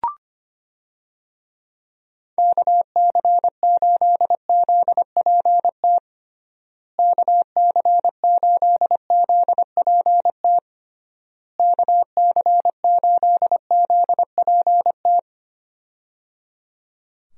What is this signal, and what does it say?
Morse code, heard as a keyed tone, about 25 wpm, 700 Hz. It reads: KC8ZPT KC8ZPT KC8ZPT
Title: USA Call Signs - 6 Characters - Repeated 3x - 25wpm